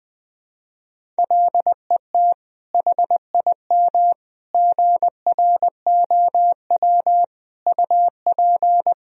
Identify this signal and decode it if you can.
Morse code, heard as a keyed tone, about 21 wpm, 700 Hz. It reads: LET HIM GROW UP
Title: Sentences from Top 300 Words and Abbreviations 20wpm